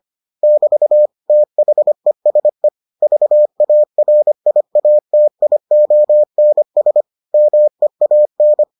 Morse code, heard as a keyed tone, about 25 wpm, 600 Hz. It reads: <BT> THESE VARIATIONS MEAN